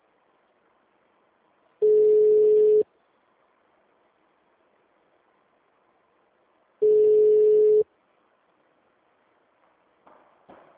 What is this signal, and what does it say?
Ringback tone (Europe, 425 Hz).
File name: out- -2016- - - .7822